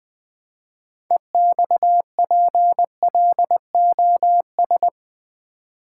Morse code, heard as a keyed tone, about 20 wpm, 700 Hz. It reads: EXPLOS
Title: Build A Word - 3 Letters At A Time - 9-Letter Words - 20wpm